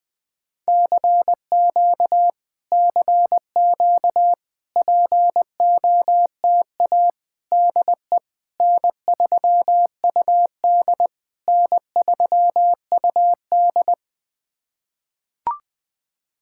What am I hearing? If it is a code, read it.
Morse code, heard as a keyed tone, about 21 wpm, 700 Hz. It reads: CQ CQ POTA DE N3UD N3UD
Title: POTA QSO Practice - 20wpm